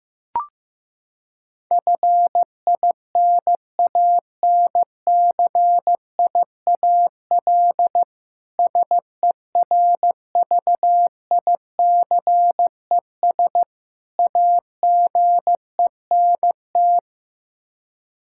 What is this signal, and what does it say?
Morse code, heard as a keyed tone, about 15 wpm, 700 Hz. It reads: FINANCIAL SERVICES AGENT
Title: Practice QSO - Exchange 3 - More Info - 15wpm